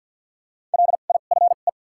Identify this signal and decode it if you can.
Morse code, heard as a keyed tone, about 54 wpm, spaced as for 37 wpm, 700 Hz. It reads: LIFE
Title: 2x Character Spacing - Top 500 Words - 50wpm